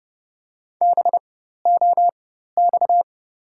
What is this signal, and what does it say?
Morse code, heard as a keyed tone, about 31 wpm, spaced as for 14 wpm, 690 Hz. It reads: BOX
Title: 4x Character Spacing - Top 500 Words - 30wpm